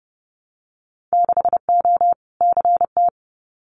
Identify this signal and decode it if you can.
Morse code, heard as a keyed tone, about 30 wpm, 700 Hz. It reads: 6O CT
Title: Winter Field Day Exchanges Encore 30wpm